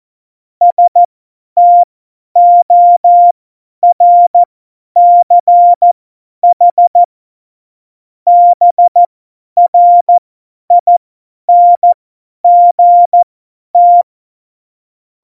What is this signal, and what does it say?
Morse code, heard as a keyed tone, about 14 wpm, spaced as for 10 wpm, 700 Hz. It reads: STORCH BRINGT
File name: Morsekurs / Witze / Witz 01. Mai 2018, 50 F70